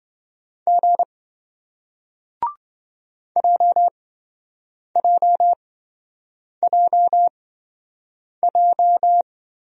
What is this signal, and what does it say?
Morse code, heard as a keyed tone, about 27 wpm, spaced as for 7 wpm, 700 Hz. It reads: G JJJJ